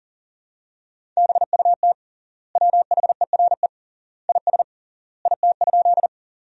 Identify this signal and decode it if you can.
Morse code, heard as a keyed tone, about 40 wpm, 700 Hz. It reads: BUT WHERE IS IT?
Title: 3x Word-Spacing - Sentences from Top 200 Words - 40wpm